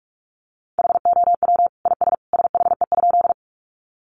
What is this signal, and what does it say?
Morse code, heard as a keyed tone, about 44 wpm, 700 Hz. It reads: HOW IS SHE?
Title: Sentences from Top 100 Words 45wpm